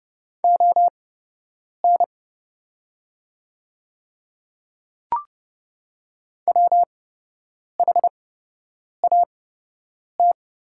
Morse code, heard as a keyed tone, about 31 wpm, spaced as for 8 wpm, 700 Hz. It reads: ON WHAT
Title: 8x Character Spacing - Top 100 Words - 30wpm